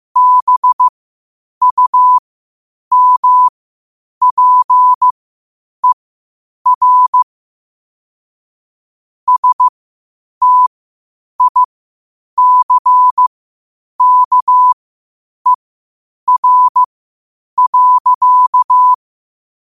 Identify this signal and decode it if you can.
Morse code, heard as a keyed tone, about 15 wpm, spaced as for 8 wpm, 1000 Hz. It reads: BUMPER STICKER.
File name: qotdfiles / 8wpm